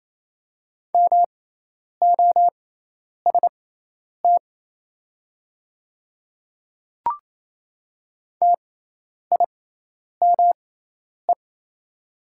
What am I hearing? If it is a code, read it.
Morse code, heard as a keyed tone, about 27 wpm, spaced as for 10 wpm, 700 Hz. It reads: MOST TIME
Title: 6x Character Spacing - Top 100 Words - 28wpm